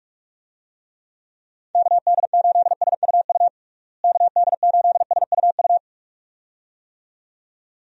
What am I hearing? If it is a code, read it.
Morse code, heard as a keyed tone, about 45 wpm, 700 Hz. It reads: KD8SUU KD8SUU
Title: USA Call Signs - 6 Characters - Repeated 2x - 45wpm